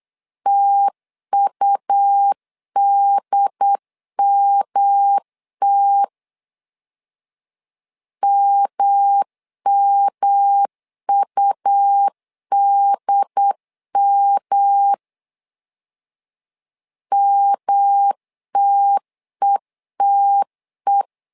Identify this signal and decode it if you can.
Morse code, heard as a keyed tone, about 8 wpm, 800 Hz. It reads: TUDMT MMUDM MTETE
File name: LEZIONI / 3 MWRO / 3 M 8wpm 2300